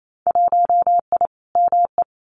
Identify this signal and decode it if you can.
Morse code, heard as a keyed tone, about 28 wpm, 700 Hz. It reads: 1I ME
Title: Winter Field Day Exchanges 28wpm